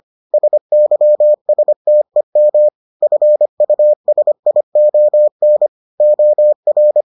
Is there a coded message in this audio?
Yes: Morse code from 0.3 to 7.0 s, heard as a keyed tone, about 25 wpm, 600 Hz. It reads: SYSTEM FUSION OR